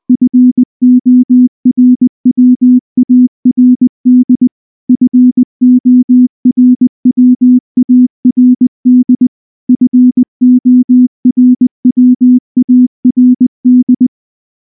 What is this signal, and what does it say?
Morse code, heard as a keyed tone, about 19 wpm, 250 Hz. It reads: FORWARD FORWARD FORWARD
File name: morse / morse-training-20WPM-10